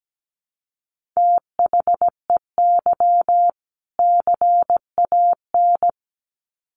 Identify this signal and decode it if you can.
Morse code, heard as a keyed tone, about 17 wpm, 700 Hz. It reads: THEY CAN
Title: Sets of 2 Words 17wpm